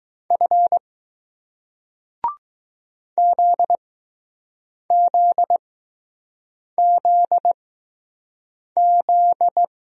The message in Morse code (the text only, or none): F ZZZZ